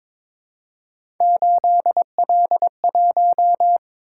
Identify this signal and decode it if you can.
Morse code, heard as a keyed tone, about 22 wpm, 700 Hz. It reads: 8L1
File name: practice / 22wpm / Three Letter-Number Combinations - 22wpm